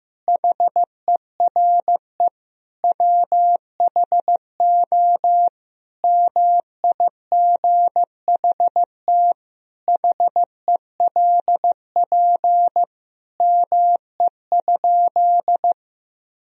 Morse code, heard as a keyed tone, about 15 wpm, 700 Hz. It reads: HERE WHO MIGHT HELP ME?
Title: Sentences from Top 1000 Words - Review Part 3 - 15wpm